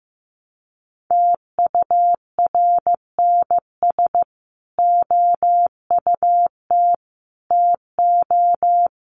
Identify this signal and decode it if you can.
Morse code, heard as a keyed tone, about 15 wpm, 700 Hz. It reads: TURNS OUT TO
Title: Sets of 5 Words Encore 15wpm